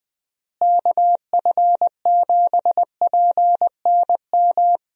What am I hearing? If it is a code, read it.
Morse code, heard as a keyed tone, about 20 wpm, 700 Hz. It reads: KF7PNM
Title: USA Call Signs - 6 Characters - Repeated 2x - 20wpm